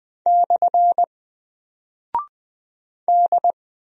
Morse code, heard as a keyed tone, about 20 wpm, 700 Hz. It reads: / D
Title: Single Character (Letter:Number:Punctuation:Prosign) 20wpm